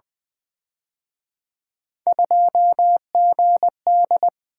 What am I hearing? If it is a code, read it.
Morse code, heard as a keyed tone, about 24 wpm, 700 Hz. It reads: 2GD
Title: Speed-Racing - Three Letter-Number Combinations - 20wpm